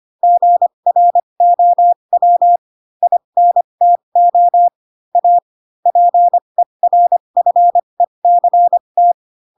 Morse code, heard as a keyed tone, about 25 wpm, spaced as for 22 wpm, 700 Hz. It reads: GROW INTO A PERFECT
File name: Pinocchio / Pinocchio ch4 18 25 ik1hge